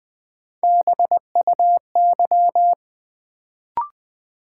Morse code, heard as a keyed tone, about 21 wpm, 700 Hz. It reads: BUY
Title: Sentences from Top 700 Words - New Word Drill - 20wpm